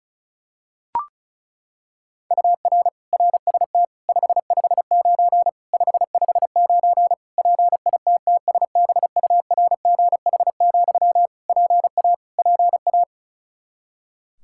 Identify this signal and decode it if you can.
Morse code, heard as a keyed tone, about 34 wpm, 700 Hz. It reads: UR RST 559 559 PITTSBURGH, PA PA
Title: Practice QSO - Exchange 1 - Beginning - 35wpm